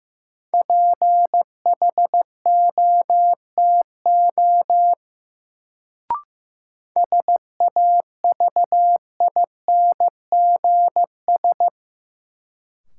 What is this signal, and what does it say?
Morse code, heard as a keyed tone, about 15 wpm, 700 Hz. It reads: PHOTO SAVINGS
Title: Sentences from Top 1200 Words - New Word Drill - 15wpm